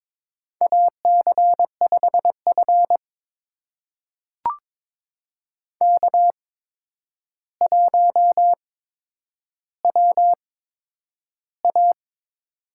Morse code, heard as a keyed tone, about 22 wpm, spaced as for 9 wpm, 700 Hz. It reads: AC5F K1WA
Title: USA Call Signs - 4 Characters - 8x Character Spacing - 22wpm